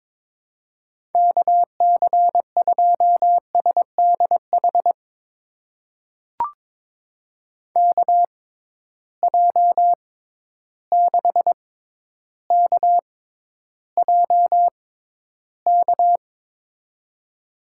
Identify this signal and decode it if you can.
Morse code, heard as a keyed tone, about 22 wpm, spaced as for 11 wpm, 700 Hz. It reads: KC2SDH KJ6KJK
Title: USA Call Signs - 6 Characters - 6x Character Spacing - 22wpm